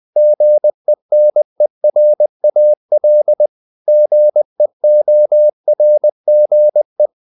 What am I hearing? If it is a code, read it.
Morse code, heard as a keyed tone, about 20 wpm, 600 Hz. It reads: GENERAL GEORGE
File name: CW / Practice / EnglishQuotations0320